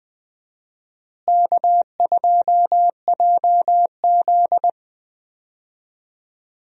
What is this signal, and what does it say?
Morse code, heard as a keyed tone, about 20 wpm, 700 Hz. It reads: K2JZ
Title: USA Call Signs - 4 Characters - Repeated 3x - 20wpm